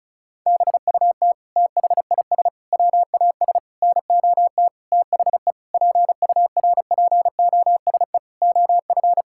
Morse code, heard as a keyed tone, about 34 wpm, 700 Hz. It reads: BUT THIS WAS NOT THE PURPOSE OF
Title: Sentences from Top 1000 Words - Review Part 6 - 35wpm